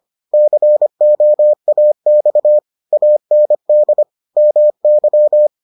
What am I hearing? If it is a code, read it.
Morse code, heard as a keyed tone, about 25 wpm, 600 Hz. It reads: COAX AND MY